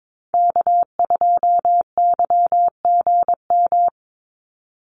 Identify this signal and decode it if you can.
Morse code, heard as a keyed tone, about 22 wpm, 690 Hz. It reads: K2YGM
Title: Call Signs Repeated 3x - 22wpm